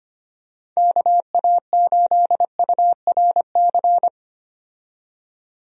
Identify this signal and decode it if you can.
Morse code, heard as a keyed tone, about 25 wpm, 700 Hz. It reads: KA8URC
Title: USA Call Signs - 6 Characters - 25wpm